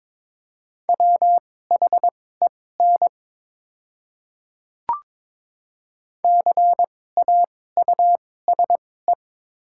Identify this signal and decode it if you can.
Morse code, heard as a keyed tone, about 22 wpm, spaced as for 16 wpm, 700 Hz. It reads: WHEN CAUSE
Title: 2x Character Spacing - Top 200 Words - 22wpm